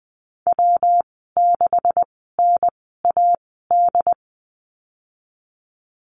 Morse code, heard as a keyed tone, about 20 wpm, spaced as for 14 wpm, 700 Hz. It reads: W6NAD
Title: 128 Lesson 21g - Number 6 - Call Signs 20wpm